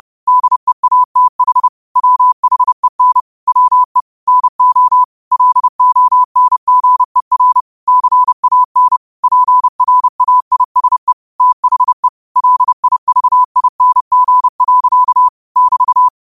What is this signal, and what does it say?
Morse code, heard as a keyed tone, about 30 wpm, 1000 Hz. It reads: DEATH WHEN WE NO LONGER CAN PRAISE THE LIVING. <BT>